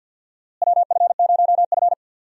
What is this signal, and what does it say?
Morse code, heard as a keyed tone, about 50 wpm, 700 Hz. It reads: WF0F
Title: USA Call Signs - 4 Characters - Repeated 2x - 50wpm